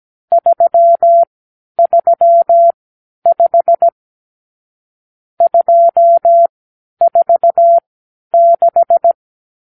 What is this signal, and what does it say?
Morse code, heard as a keyed tone, about 17 wpm, spaced as for 10 wpm, 680 Hz. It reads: 335 246
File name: k7qo / Lesson 50 (1-6)